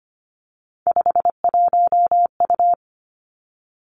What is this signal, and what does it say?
Morse code, heard as a keyed tone, about 25 wpm, 700 Hz. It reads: 51U
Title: Three Challenging Letter-Number Combinations 25wpm